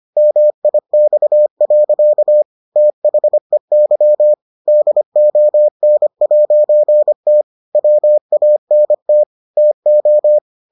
Morse code, heard as a keyed tone, about 25 wpm, 600 Hz. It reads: MIX. THEY DON'T WANT TO